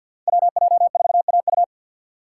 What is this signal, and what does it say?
Morse code, heard as a keyed tone, about 46 wpm, 700 Hz. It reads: WJ4AU
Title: Code Groups - 5 Letter:Numbers - 50wpm